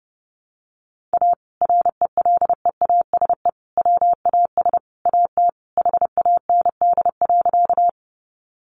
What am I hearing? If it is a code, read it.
Morse code, heard as a keyed tone, about 29 wpm, 700 Hz. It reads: A RELEASE WAS AT HAND.
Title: Sentences from Top 1100 Words 30wpm